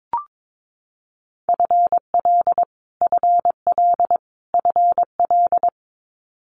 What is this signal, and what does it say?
Morse code, heard as a keyed tone, about 22 wpm, 700 Hz. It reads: FL FL FL
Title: Multi-letter Phonemes Repeated 3x 22wpm